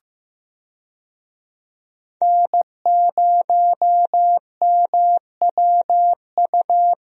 Morse code, heard as a keyed tone, about 15 wpm, 700 Hz. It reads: N0MWU
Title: USA Call Signs - 5 Characters - Repeated 3x - 15wpm